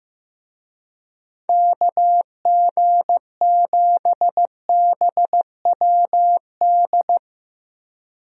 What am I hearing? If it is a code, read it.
Morse code, heard as a keyed tone, about 15 wpm, 700 Hz. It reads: KG7BWD
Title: USA Call Signs - 6 Characters - Repeated 3x - 15wpm